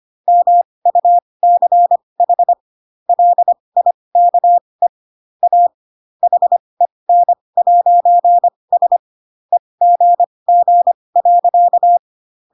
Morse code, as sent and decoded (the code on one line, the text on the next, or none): -- ..- -.-. .... / .-.. .. -.- . / .- / .... -. .----. ... / . --. --. .-.-.-
MUCH LIKE A HEN'S EGG.